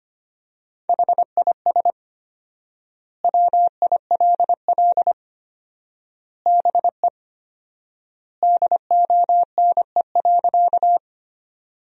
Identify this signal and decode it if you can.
Morse code, heard as a keyed tone, about 25 wpm, 700 Hz. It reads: HIS WILL BE DONE.
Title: 4x Word-Spacing - Sentences from Top 500 Words - 25wpm